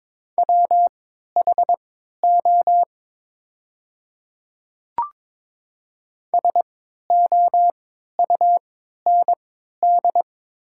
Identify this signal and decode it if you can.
Morse code, heard as a keyed tone, about 22 wpm, spaced as for 12 wpm, 700 Hz. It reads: WHO SOUND
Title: 3x Character Spacing - Top 300 Words - 22wpm